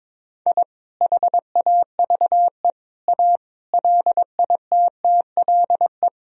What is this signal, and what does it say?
Morse code, heard as a keyed tone, about 21 wpm, 700 Hz. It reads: I HAVE A LITTLE
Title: Sentences from Top 1000 Words - Review Part 1 - 22wpm